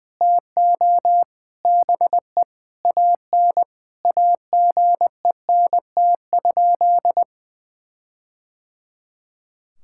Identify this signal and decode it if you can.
Morse code, heard as a keyed tone, about 20 wpm, 700 Hz. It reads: TO BE AN AGENT?
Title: Sentences from Top 1600 Words 20wpm